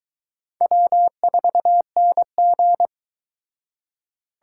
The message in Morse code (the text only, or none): W4NG